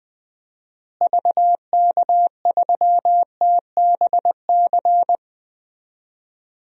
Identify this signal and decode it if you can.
Morse code, heard as a keyed tone, about 20 wpm, 700 Hz. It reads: VK3TBC
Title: DXCC-100 Call Signs - 6 Characters - Repeated 3x - 20wpm